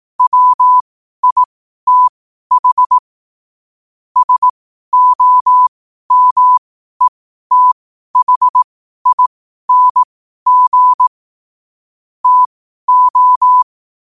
Morse code, heard as a keyed tone, about 18 wpm, spaced as for 12 wpm, 1000 Hz. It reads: WITH SOMETHING TO
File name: qotdfiles / 12wpm